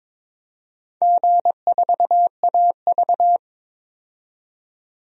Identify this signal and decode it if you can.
Morse code, heard as a keyed tone, about 22 wpm, 700 Hz. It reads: G4AV
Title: DXCC-100 Call Signs - 4 Characters - 22wpm